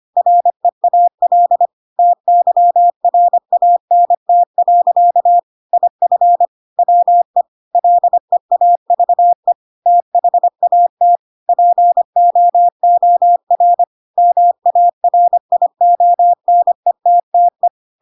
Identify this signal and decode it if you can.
Morse code, heard as a keyed tone, about 25 wpm, 700 Hz. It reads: REAL TYRANT. IF WE LEAVE THAT POOR MARIONETTE
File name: Pinocchio / Pinocchio ch3 25 25 ik1hge